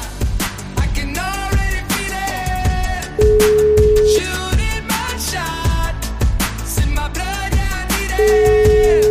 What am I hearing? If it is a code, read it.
Ringback tone (Europe, 425 Hz), with noise.